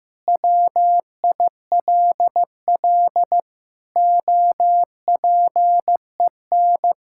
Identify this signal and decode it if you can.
Morse code, heard as a keyed tone, about 15 wpm, 700 Hz. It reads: WILL OPEN
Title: Sentences from Top 1600 Words 15wpm